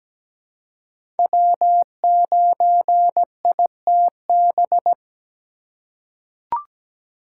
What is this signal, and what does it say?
Morse code, heard as a keyed tone, about 17 wpm, 700 Hz. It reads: W9ITB
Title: USA Call Signs - 4x Character Spacing - 17wpm